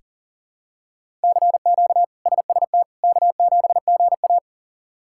Morse code, heard as a keyed tone, about 40 wpm, 700 Hz. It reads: CQ SST K7GA
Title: K1USN SST QSO Practice - 40wpm